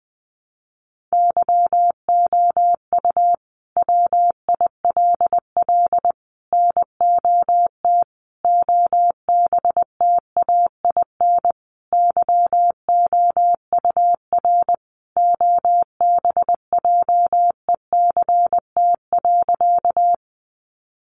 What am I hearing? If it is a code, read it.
Morse code, heard as a keyed tone, about 20 wpm, 700 Hz. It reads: YOU WILL NOT OBTAIN YOUR OBJECT.
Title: Sentences from Top 1300 Words 20wpm